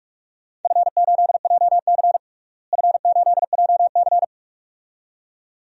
Morse code, heard as a keyed tone, about 41 wpm, 700 Hz. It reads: F8JC F8JC
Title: DXCC-100 Call Signs - 4 Characters - Repeated 2x - 45wpm